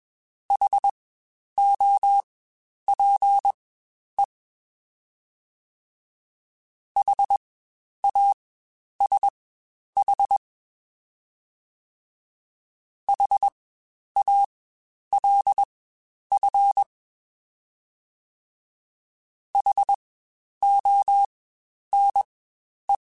Morse code, heard as a keyed tone, about 21 wpm, spaced as for 10 wpm, 800 Hz. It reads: HOPE HASH HALF HONE
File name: Spaced Alphas / Spaced Alphas mp3s / Hs sp @20